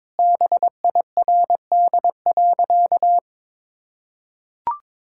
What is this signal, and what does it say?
Morse code, heard as a keyed tone, about 22 wpm, 700 Hz. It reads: BIRD.
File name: practice / 22wpm / Sentences with 3-Word Phrases - 22wpm